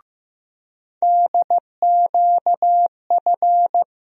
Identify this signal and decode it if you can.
Morse code, heard as a keyed tone, about 15 wpm, 700 Hz. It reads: DQF
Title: USA Call Sign Postfixes - 3 Letters 15wpm